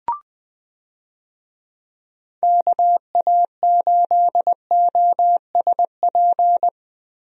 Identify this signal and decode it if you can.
Morse code, heard as a keyed tone, about 20 wpm, 700 Hz. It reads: KA8OSP
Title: USA Call Signs - 6 Characters - Repeated 2x - 20wpm